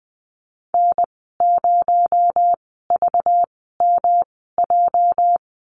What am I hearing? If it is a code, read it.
Morse code, heard as a keyed tone, about 20 wpm, spaced as for 14 wpm, 700 Hz. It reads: N0VMJ
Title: 245 Lesson 39g - Letter z - Call Signs 20wpm